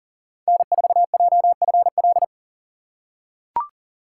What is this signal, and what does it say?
Morse code, heard as a keyed tone, about 40 wpm, 700 Hz. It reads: N4JFL
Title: USA Call Signs - 5 Characters - 3x Character Spacing - 40wpm